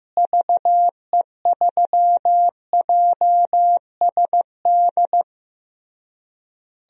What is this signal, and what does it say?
Morse code, heard as a keyed tone, about 15 wpm, 700 Hz. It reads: VE3JSD
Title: DXCC-100 Call Signs - 6 Characters - Repeated 3x - 15wpm